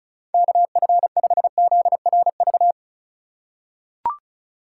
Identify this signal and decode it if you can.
Morse code, heard as a keyed tone, about 35 wpm, 700 Hz. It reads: KF5ZRV
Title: USA Call Signs - 6 Characters - 3x Character Spacing - 35wpm